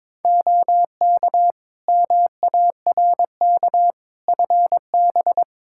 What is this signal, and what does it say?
Morse code, heard as a keyed tone, about 22 wpm, 700 Hz. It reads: OK MARK FB
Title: Practice QSO - Exchange 2 - Info - 22wpm